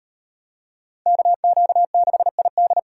Morse code, heard as a keyed tone, about 38 wpm, 700 Hz. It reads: KQ6ID